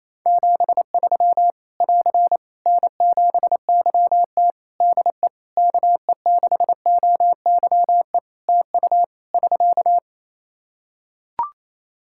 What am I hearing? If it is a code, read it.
Morse code, heard as a keyed tone, about 28 wpm, 700 Hz. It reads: 73 <AR> N7YT DE KE6OYE TU <SK>
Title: Practice QSO - Exchange 4 - Ending - 28wpm